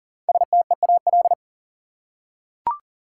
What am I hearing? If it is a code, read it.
Morse code, heard as a keyed tone, about 40 wpm, 700 Hz. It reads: STEAL
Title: Five Letter Words - 40wpm